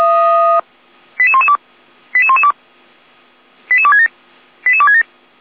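Five-tone selective call (ZVEI), heard as quick runs of five tones, tones of about 70 ms. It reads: 88112 88112 88267 88267
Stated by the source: ZVEI five-tone selective call 88485 received on an SDR.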